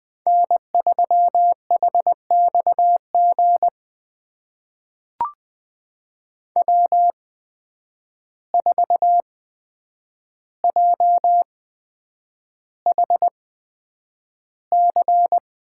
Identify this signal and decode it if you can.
Morse code, heard as a keyed tone, about 20 wpm, spaced as for 9 wpm, 700 Hz. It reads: N3HXG W4JHC
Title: USA Call Signs - 5 Characters - 8x Character Spacing - 20wpm